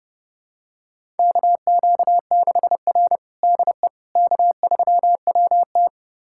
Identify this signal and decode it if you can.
Morse code, heard as a keyed tone, about 30 wpm, 700 Hz. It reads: KQ6R DE K3WT
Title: USA Callsign DE Callsign - 4 Characters - 30wpm